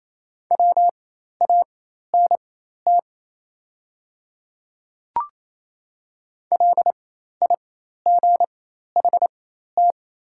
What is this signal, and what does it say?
Morse code, heard as a keyed tone, about 27 wpm, spaced as for 13 wpm, 700 Hz. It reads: WANT LIGHT
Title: 4x Character Spacing - Top 200 Words - 28wpm